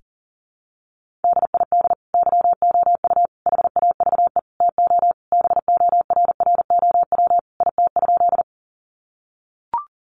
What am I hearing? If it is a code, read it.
Morse code, heard as a keyed tone, about 37 wpm, 700 Hz. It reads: DID YOU HAVE TO BORROW IT?
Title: Sentences from Top 1500 Words 40wpm